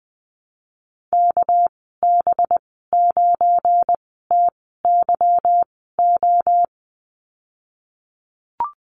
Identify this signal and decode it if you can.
Morse code, heard as a keyed tone, about 19 wpm, spaced as for 14 wpm, 700 Hz. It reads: KB9TYO
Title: 238 Lesson 38g - Letter q - Call Signs 20wpm